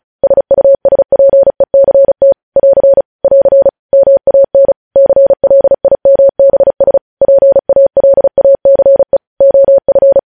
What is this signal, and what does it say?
Morse code, heard as a keyed tone, about 36 wpm, 550 Hz. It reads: SUSPECT <AR> <AR> MAN CLIMBS PALACE OF
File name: podcast / News headlines-1x-35x35